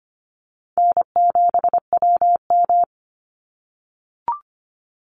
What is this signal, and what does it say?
Morse code, heard as a keyed tone, about 25 wpm, 700 Hz. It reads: N7WM
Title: NA SOTA Chaser Callsigns 25wpm